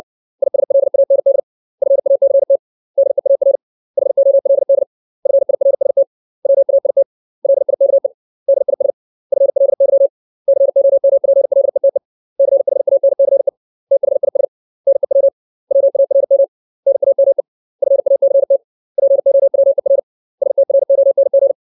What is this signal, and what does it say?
Morse code, heard as a keyed tone, about 60 wpm, 550 Hz. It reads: ISLAND FACT DEAR HOLD FIRST WAIT LIKE DIS FLY COMPLETE CHANCE THIS NEW WARM NAME FACT POWER STRONG
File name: 632 words / 632 Words 60WPM